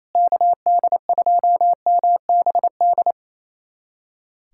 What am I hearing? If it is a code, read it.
Morse code, heard as a keyed tone, about 28 wpm, 700 Hz. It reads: KD2MBD
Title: USA Callsign DE Callsign - 6 Characters - 28wpm